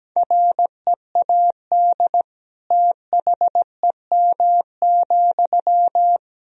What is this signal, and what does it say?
Morse code, heard as a keyed tone, about 17 wpm, 700 Hz. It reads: READ THEM,
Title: Sentences from Top 400 Words and Abbreviations 17wpm